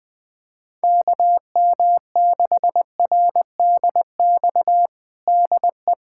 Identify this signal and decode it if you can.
Morse code, heard as a keyed tone, about 20 wpm, 700 Hz. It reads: KM6RDX DE
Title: USA Callsign DE Callsign - 6 Characters - 20wpm